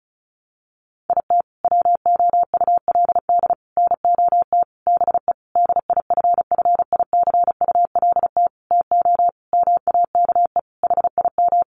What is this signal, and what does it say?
Morse code, heard as a keyed tone, about 34 wpm, 700 Hz. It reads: IT WOULD NOT BE DIFFICULT TO MAKE HIM
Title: Sentences from Top 600 Words 35wpm